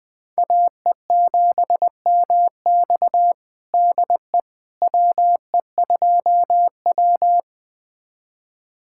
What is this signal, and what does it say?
Morse code, heard as a keyed tone, about 20 wpm, 700 Hz. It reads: AE7MX DE WE2W
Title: USA Callsign DE Callsign 20wpm